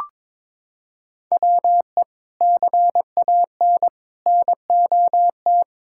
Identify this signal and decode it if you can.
Morse code, heard as a keyed tone, about 22 wpm, 700 Hz. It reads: WE CAN NOT
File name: practice / 22wpm / Sets of 8 Words - 22wpm